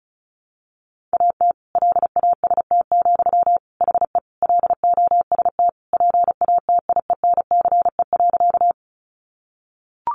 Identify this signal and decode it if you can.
Morse code, heard as a keyed tone, about 34 wpm, 700 Hz. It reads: AT LAST, HE LOST PATIENCE.
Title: Sentences from Top 1400 Words 35wpm